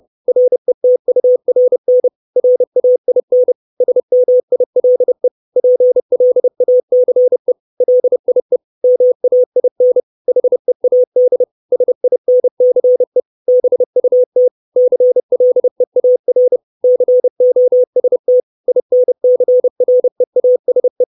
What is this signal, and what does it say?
Morse code, heard as a keyed tone, about 30 wpm, 500 Hz. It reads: RETURN RAIN SMILE PLACE LIE MAIN HEAD SINCE BUT CLEAR COST INCREASE
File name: 632 Words 30WPM